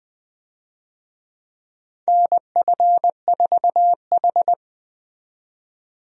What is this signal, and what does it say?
Morse code, heard as a keyed tone, about 20 wpm, 700 Hz. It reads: NF4H
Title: USA Call Signs - 4 Characters - 20wpm